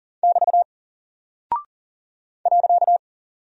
Morse code, heard as a keyed tone, about 39 wpm, 700 Hz. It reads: <BT> .
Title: Single Character (Letter:Number:Punctuation:Prosign) Alt 40wpm